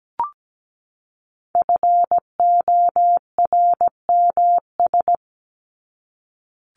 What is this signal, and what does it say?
Morse code, heard as a keyed tone, about 17 wpm, 700 Hz. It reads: FORMS
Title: ES:S Suffix - Top 100 Words with Base Words 17wpm